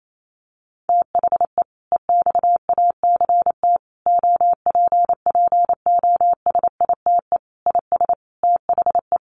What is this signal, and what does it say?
Morse code, heard as a keyed tone, about 28 wpm, 700 Hz. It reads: THE EXACT OPPOSITE IS THE
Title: Sentences from Top 1300 Words 28wpm